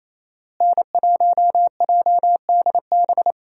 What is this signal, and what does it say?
Morse code, heard as a keyed tone, about 28 wpm, 700 Hz. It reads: N1JDB
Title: USA Call Signs - 5 Characters - Repeated 2x - 28wpm